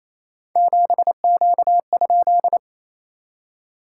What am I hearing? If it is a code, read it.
Morse code, heard as a keyed tone, about 28 wpm, 690 Hz. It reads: 7Q?
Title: SOTA QSO Practice 28wpm